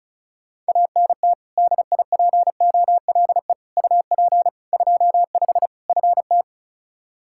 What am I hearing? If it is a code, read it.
Morse code, heard as a keyed tone, about 36 wpm, 700 Hz. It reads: ANT DIPOLE UP 25 FT
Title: K1USN SST QSO Practice 35wpm